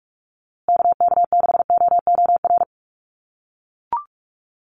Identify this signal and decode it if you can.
Morse code, heard as a keyed tone, about 44 wpm, 700 Hz. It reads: KK6OOR
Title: USA Call Signs Repeated 3x 45wpm